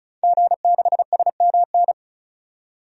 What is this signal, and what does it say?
Morse code, heard as a keyed tone, about 34 wpm, 700 Hz. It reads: G6SMN
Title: DXCC-100 Call Signs - 5 Characters - 35wpm